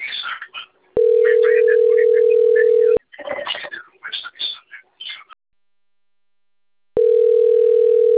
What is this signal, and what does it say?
Ringback tone (North America), 440 + 480 Hz, with speech.
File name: out- - - - - .12053